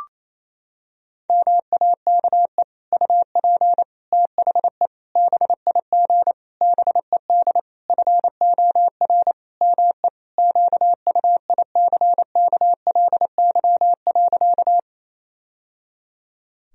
Morse code, heard as a keyed tone, about 28 wpm, 700 Hz. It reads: MAKE UP THE BIG BED FOR ME QUICKLY.
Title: Sentences from Top 1000 Words - Review Part 1 - 28wpm